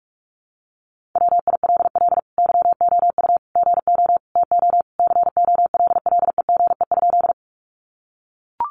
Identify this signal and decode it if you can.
Morse code, heard as a keyed tone, about 44 wpm, 700 Hz. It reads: WILL YOU GO TO COLLEGE?
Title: Sentences from Top 1300 Words 45wpm